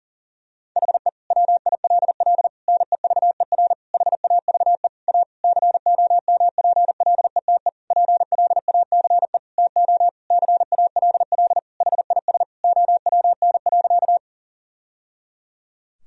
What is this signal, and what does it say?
Morse code, heard as a keyed tone, about 40 wpm, 700 Hz. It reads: HE WILL NEVER HAVE A COMPLETE PLACE TO CALL HIS OWN.
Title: Sentences from Top 1000 Words - Review Part 2 - 40wpm